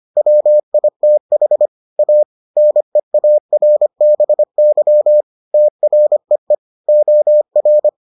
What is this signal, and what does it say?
Morse code, heard as a keyed tone, about 25 wpm, 600 Hz. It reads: WITH A NEARBY TREE OR